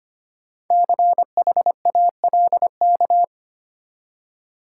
Morse code, heard as a keyed tone, about 25 wpm, 700 Hz. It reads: CHALK
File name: practice / 25wpm / Easy Words with Hints - 25wpm